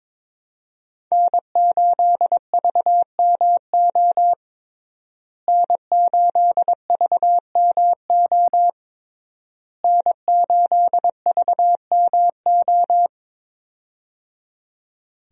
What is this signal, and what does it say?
Morse code, heard as a keyed tone, about 21 wpm, 700 Hz. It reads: N8VMO N8VMO N8VMO
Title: USA Call Signs - 5 Characters - Repeated 3x - 22wpm